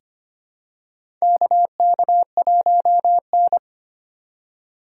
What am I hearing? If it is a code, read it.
Morse code, heard as a keyed tone, about 25 wpm, 700 Hz. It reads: KK1N
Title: USA Call Signs - 2x Character Spacing - 25wpm